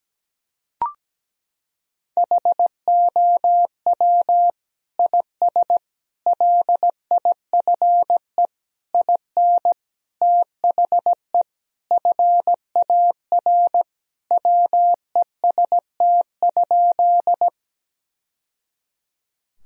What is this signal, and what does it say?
Morse code, heard as a keyed tone, about 17 wpm, 700 Hz. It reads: HOW IS LIFE IN THE FAR WEST?
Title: Sentences from Top 1000 Words - Review Part 2 - 17wpm